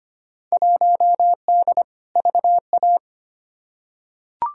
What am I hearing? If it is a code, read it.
Morse code, heard as a keyed tone, about 25 wpm, 700 Hz. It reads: 1D VA
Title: Field Day Exchanges - 25wpm